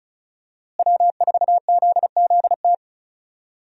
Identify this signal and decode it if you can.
Morse code, heard as a keyed tone, about 36 wpm, 700 Hz. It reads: W4ZZT
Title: USA Call Signs - 5 Characters - Repeated 3x - 35wpm